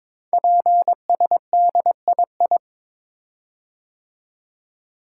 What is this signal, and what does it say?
Morse code, heard as a keyed tone, about 22 wpm, 700 Hz. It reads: PSDII
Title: Code Groups - 5 Letter:Numbers - 22wpm